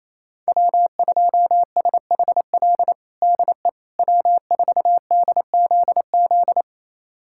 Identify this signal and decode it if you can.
Morse code, heard as a keyed tone, about 28 wpm, 700 Hz. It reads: W2SHL DE W4DZZ
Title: USA Callsign DE Callsign - 5 Characters - 28wpm